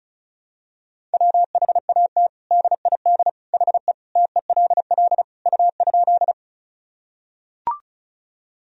Morse code, heard as a keyed tone, about 36 wpm, 700 Hz. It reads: WHAT DID HE TELL U?
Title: Sentences from Top 200 Words and Abbreviations 35wpm